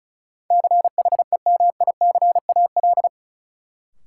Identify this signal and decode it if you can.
Morse code, heard as a keyed tone, about 34 wpm, 700 Hz. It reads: CHEMICAL
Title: Sentences from Top 1200 Words - New Word Drill - 35wpm